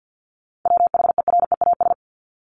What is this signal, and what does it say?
Morse code, heard as a keyed tone, about 50 wpm, 700 Hz. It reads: WHEREAS
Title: Top 1501-1750 Words 50wpm